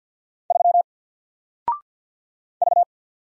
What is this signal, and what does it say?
Morse code, heard as a keyed tone, about 50 wpm, 700 Hz. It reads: 3 V
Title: Single Letter-Number 50wpm